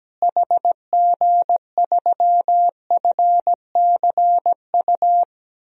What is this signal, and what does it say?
Morse code, heard as a keyed tone, about 17 wpm, 700 Hz. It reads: HG3FCU
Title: DXCC-100 Call Signs - 6 Characters - Repeated 3x - 17wpm